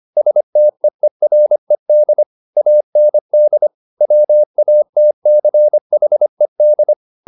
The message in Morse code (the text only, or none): STEERED AND WATCHED